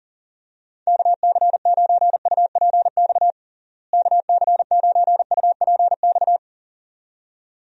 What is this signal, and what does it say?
Morse code, heard as a keyed tone, about 40 wpm, 700 Hz. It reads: KC9UPX KC9UPX
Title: USA Call Signs - 6 Characters - Repeated 2x - 40wpm